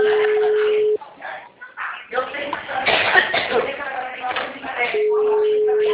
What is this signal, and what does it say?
Ringback tone (Europe, 425 Hz), with speech.
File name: out- -2008- - - .22317